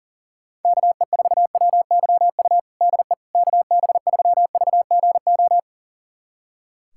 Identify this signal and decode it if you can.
Morse code, heard as a keyed tone, about 40 wpm, 700 Hz. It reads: KE4WYU DE KB3VGO
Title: USA Callsign DE Callsign - 6 Characters - 40wpm